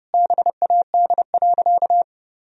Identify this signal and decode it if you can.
Morse code, heard as a keyed tone, about 30 wpm, 700 Hz. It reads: BAD.
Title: Sentences from Top 1000 Words - Review Part 2 - 30wpm